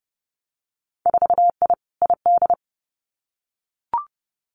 Morse code, heard as a keyed tone, about 30 wpm, 700 Hz. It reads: 4I ID
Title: Winter Field Day Exchanges 30wpm